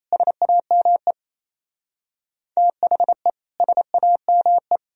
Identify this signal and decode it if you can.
Morse code, heard as a keyed tone, about 34 wpm, 700 Hz. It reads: SAME THE SAME
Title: Speed-Racing - Sets of 2 Words 28wpm